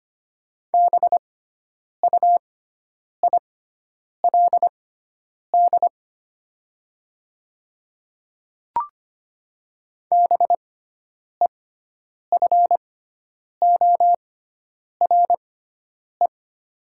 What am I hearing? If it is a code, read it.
Morse code, heard as a keyed tone, about 25 wpm, spaced as for 9 wpm, 700 Hz. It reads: BUILD BEFORE
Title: 6x Character Spacing - Top 300 Words - 25wpm